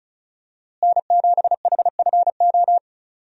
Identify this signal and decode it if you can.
Morse code, heard as a keyed tone, about 36 wpm, 700 Hz. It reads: N7HFO
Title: USA Call Signs - 5 Characters - Repeated 3x - 35wpm